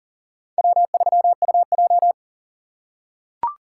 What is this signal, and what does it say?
Morse code, heard as a keyed tone, about 40 wpm, 700 Hz. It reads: W3UJ
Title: USA Call Signs - 4 Characters - 4x Character Spacing - 40wpm